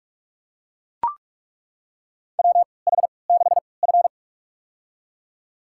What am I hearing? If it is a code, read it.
Morse code, heard as a keyed tone, about 41 wpm, spaced as for 25 wpm, 700 Hz. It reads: WH6F
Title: USA Call Signs - 4 Characters - 3x Character Spacing - 45wpm